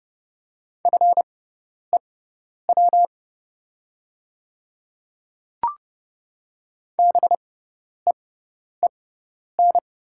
Morse code, heard as a keyed tone, about 31 wpm, spaced as for 10 wpm, 700 Hz. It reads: FEW BEEN
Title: 6x Character Spacing - Top 300 Words - 30wpm